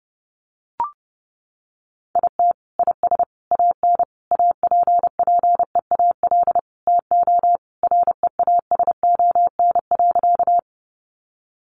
Morse code, heard as a keyed tone, about 30 wpm, 700 Hz. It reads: IT IS AN APPEAL TO REASON.
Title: Sentences from Top 1300 Words 30wpm